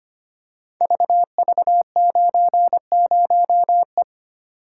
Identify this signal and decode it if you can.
Morse code, heard as a keyed tone, about 25 wpm, 700 Hz. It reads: VV90E
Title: Code Groups - 5 Letter:Numbers - 25wpm